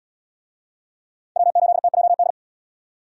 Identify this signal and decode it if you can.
Morse code, heard as a keyed tone, about 64 wpm, 700 Hz. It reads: W8TJD